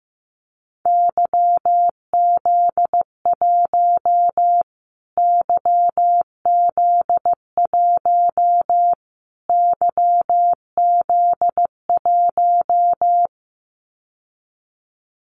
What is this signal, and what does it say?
Morse code, heard as a keyed tone, about 15 wpm, 700 Hz. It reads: YZ1 YZ1 YZ1
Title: Callsign Prefixes for Top 25 DXCC - Repeated 3x - 15wpm